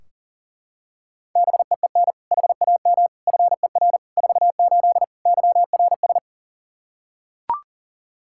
Morse code, heard as a keyed tone, about 40 wpm, 700 Hz. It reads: BEEN HAM FER 48 YRS
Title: Practice QSO - Exchange 3 - More Info - 40wpm